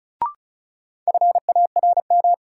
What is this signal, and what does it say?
Morse code, heard as a keyed tone, about 36 wpm, 700 Hz. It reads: FARM
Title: Sentences from Top 1000 Words - Review Part 2 - New Word Drill - 35wpm